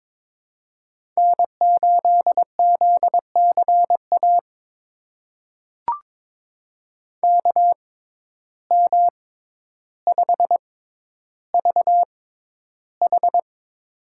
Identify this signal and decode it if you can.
Morse code, heard as a keyed tone, about 22 wpm, spaced as for 11 wpm, 700 Hz. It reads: N8ZCA KM5VH
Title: USA Call Signs - 5 Characters - 6x Character Spacing - 22wpm